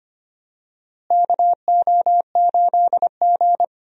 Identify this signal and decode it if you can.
Morse code, heard as a keyed tone, about 25 wpm, 700 Hz. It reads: KO8G
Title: USA Call Signs - 4 Characters - Repeated 2x - 25wpm